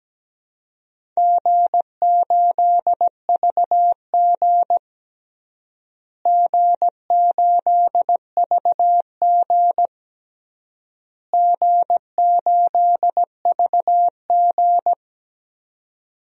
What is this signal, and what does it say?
Morse code, heard as a keyed tone, about 17 wpm, 700 Hz. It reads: G8VG G8VG G8VG
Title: DXCC-100 Call Signs - 4 Characters - Repeated 3x - 17wpm